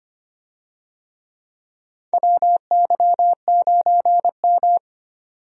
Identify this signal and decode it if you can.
Morse code, heard as a keyed tone, about 25 wpm, 700 Hz. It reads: WY9M
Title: USA Call Signs - 4 Characters - 25wpm